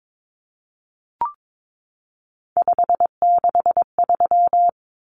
Morse code, heard as a keyed tone, about 22 wpm, 700 Hz. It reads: 563
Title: Numbers - 3-Digit 22wpm